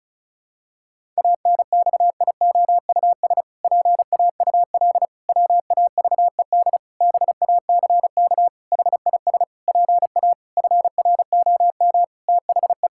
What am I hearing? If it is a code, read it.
Morse code, heard as a keyed tone, about 36 wpm, 700 Hz. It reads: ANXIOUS PAUL WAVED BACK HIS PA FROM THE
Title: Pangram Challenge v2 - 35wpm